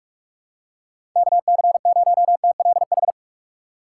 Morse code, heard as a keyed tone, about 46 wpm, 690 Hz. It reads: KC0TLH
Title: USA Call Signs - 6 Characters - 2x Character Spacing - 45wpm